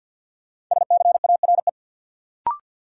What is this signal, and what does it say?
Morse code, heard as a keyed tone, about 51 wpm, 700 Hz. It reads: SCARE
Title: Sentences from Top 1000 Words - Review Part 5 - New Word Drill - 50wpm